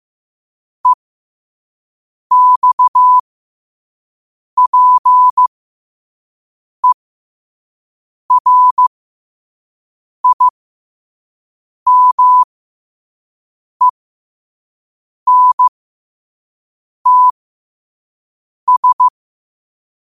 Morse code, heard as a keyed tone, about 15 wpm, spaced as for 5 wpm, 1000 Hz. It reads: EXPERIMENTS
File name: qotdfiles / 5wpm